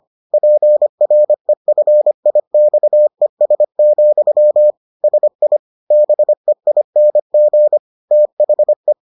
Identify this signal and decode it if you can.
Morse code, heard as a keyed tone, about 25 wpm, 600 Hz. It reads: PREFIXES, SI BEING THE